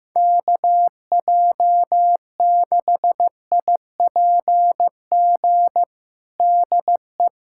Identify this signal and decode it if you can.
Morse code, heard as a keyed tone, about 15 wpm, 700 Hz. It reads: KJ6IPG DE
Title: USA Callsign DE Callsign 15wpm